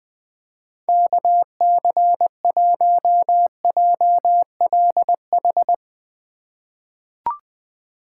Morse code, heard as a keyed tone, about 20 wpm, 700 Hz. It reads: KC1JLH
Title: USA Call Signs - 6 Characters - 2x Character Spacing - 20wpm